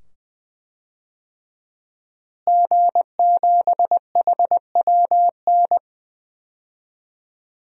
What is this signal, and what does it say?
Morse code, heard as a keyed tone, about 20 wpm, 700 Hz. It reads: G7HWN
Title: DXCC-100 Call Signs - 5 Characters - Repeated 2x - 20wpm